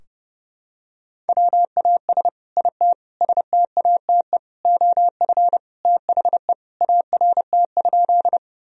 Morse code, heard as a keyed tone, about 30 wpm, 700 Hz. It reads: WAS IT STATE OF THE ART?
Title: Sentences from Top 1000 Words - Review Part 3 - 30wpm